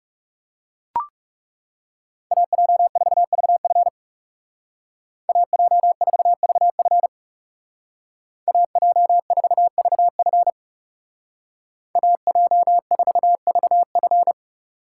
Morse code, heard as a keyed tone, about 35 wpm, 700 Hz. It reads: AJ4VF AJ4VF AJ4VF AJ4VF
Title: Speed Racing - USA 2x2 Call Signs 30wpm